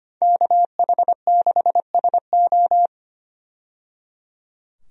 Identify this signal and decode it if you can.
Morse code, heard as a keyed tone, about 25 wpm, 700 Hz. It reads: KH6SO
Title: DXCC-100 Call Signs - 5 Characters - 25wpm